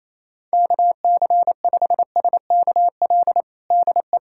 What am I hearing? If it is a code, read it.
Morse code, heard as a keyed tone, about 28 wpm, 700 Hz. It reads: KC5SKL DE
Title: USA Callsign DE Callsign - 6 Characters - 28wpm